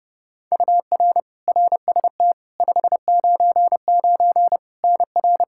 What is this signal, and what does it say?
Morse code, heard as a keyed tone, about 32 wpm, 700 Hz. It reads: UR RST 599 NR
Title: K1USN SST QSO Practice 30wpm